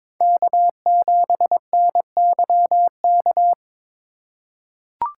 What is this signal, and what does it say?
Morse code, heard as a keyed tone, about 22 wpm, 700 Hz. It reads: K7NYK
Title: USA Call Signs - 3x Character Spacing - 22wpm